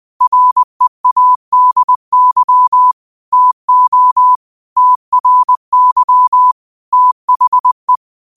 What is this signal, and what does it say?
Morse code, heard as a keyed tone, about 20 wpm, 1000 Hz. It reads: READY TO TRY THE
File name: qotdfiles / 20wpm